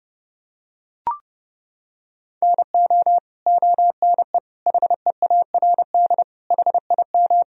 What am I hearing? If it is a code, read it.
Morse code, heard as a keyed tone, about 30 wpm, 700 Hz. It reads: NO ONE HEARD HIM
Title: Sentences from Top 1000 Words - Review Part 1 - 30wpm